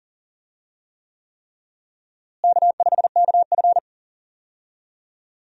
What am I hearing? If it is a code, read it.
Morse code, heard as a keyed tone, about 40 wpm, 700 Hz. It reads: K5KF
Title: USA Call Signs - 4 Characters - Repeated 3x - 40wpm